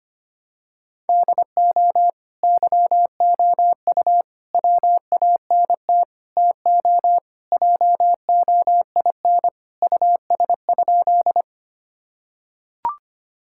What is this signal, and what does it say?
Morse code, heard as a keyed tone, about 25 wpm, 700 Hz. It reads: DO YOU WANT TO JOIN US?
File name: practice / 25wpm / Sentences with 4-Word Phrases - 25wpm